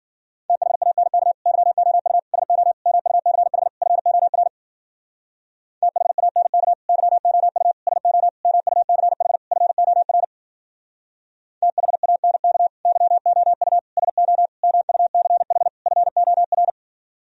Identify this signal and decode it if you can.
Morse code, heard as a keyed tone, about 49 wpm, 700 Hz. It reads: THANK YOU SO MUCH FOR THANK YOU SO MUCH FOR THANK YOU SO MUCH FOR
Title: Speed-Racing - Sets of 5 Words 40wpm